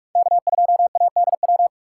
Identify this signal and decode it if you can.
Morse code, heard as a keyed tone, about 43 wpm, 700 Hz. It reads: K2ADW